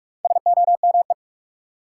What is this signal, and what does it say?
Morse code, heard as a keyed tone, about 44 wpm, 700 Hz. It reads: SOME
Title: Four Letter Words - 45wpm